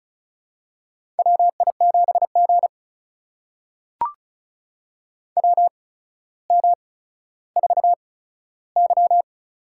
Morse code, heard as a keyed tone, about 36 wpm, spaced as for 15 wpm, 700 Hz. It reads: WI7G WM4Y
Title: USA Call Signs - 4 Characters - 8x Character Spacing - 35wpm